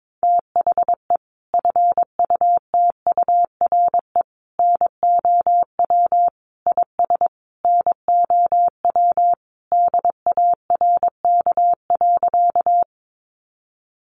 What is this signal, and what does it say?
Morse code, heard as a keyed tone, about 22 wpm, 700 Hz. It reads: THE FUTURE NOW IS NOW DARK.
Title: Sentences from Top 700 Words 22wpm